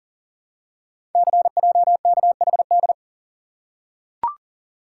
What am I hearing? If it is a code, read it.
Morse code, heard as a keyed tone, about 39 wpm, 700 Hz. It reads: CJKHD
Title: Code Groups - 5 Letter:Numbers - 40wpm